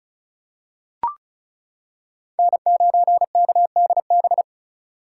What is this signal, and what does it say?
Morse code, heard as a keyed tone, about 34 wpm, 700 Hz. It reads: N9KDB
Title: K1USN SST QSO Practice - 35wpm